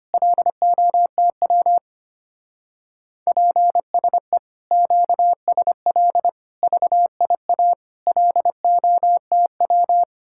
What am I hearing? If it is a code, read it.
Morse code, heard as a keyed tone, about 32 wpm, 700 Hz. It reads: LOTW PSE QSL VIA LOTW